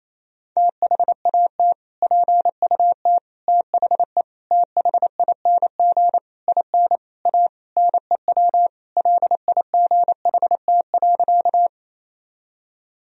Morse code, heard as a keyed tone, about 28 wpm, 700 Hz. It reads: THAT PUT THE THING IN A NEW LIGHT.
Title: Sentences from Top 1000 Words - Review Part 1 - 28wpm